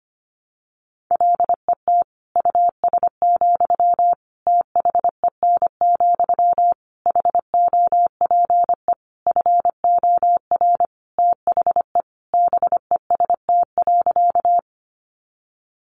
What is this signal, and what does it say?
Morse code, heard as a keyed tone, about 25 wpm, 700 Hz. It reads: LET US, THEN, HOPE FOR THE BEST.
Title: Sentences from Top 1000 Words 25wpm